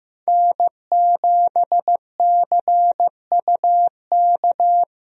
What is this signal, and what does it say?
Morse code, heard as a keyed tone, about 16 wpm, 700 Hz. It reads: N7CUK
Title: USA Call Signs - 5 Characters - Repeated 3x - 15wpm